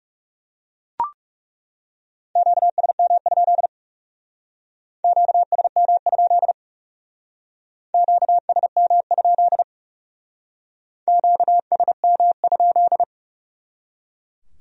Morse code, heard as a keyed tone, about 35 wpm, 700 Hz. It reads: QSM? QSM? QSM? QSM?
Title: Speed-Racing Q-Codes - 30wpm